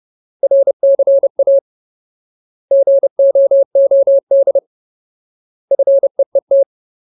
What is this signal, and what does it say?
Morse code, heard as a keyed tone, about 30 wpm, 550 Hz. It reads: RCA GOOD FEET
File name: Ultimate Word List / 30WPM Ultimate